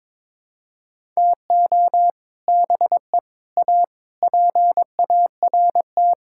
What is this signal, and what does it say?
Morse code, heard as a keyed tone, about 22 wpm, 700 Hz. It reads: TO BE A PART
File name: practice / 22wpm / Sets of 6 Words - 22wpm